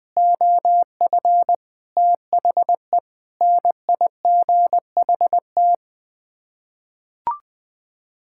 Morse code, heard as a keyed tone, about 21 wpm, 700 Hz. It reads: OF THE NIGHT
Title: Sets of 8 Words - 20wpm